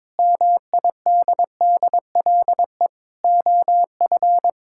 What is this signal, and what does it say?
Morse code, heard as a keyed tone, about 22 wpm, 700 Hz. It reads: MIDDLE OF
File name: Sentences with 7-Word Phrases - 22wpm